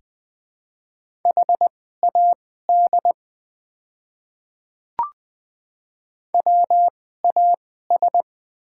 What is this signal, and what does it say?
Morse code, heard as a keyed tone, about 20 wpm, spaced as for 15 wpm, 700 Hz. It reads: HAD WAS
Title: 2x Character Spacing - Top 200 Words - 20wpm